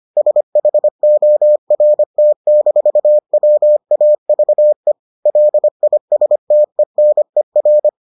Morse code, heard as a keyed tone, about 25 wpm, 600 Hz. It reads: SHORT-WAVE LISTENER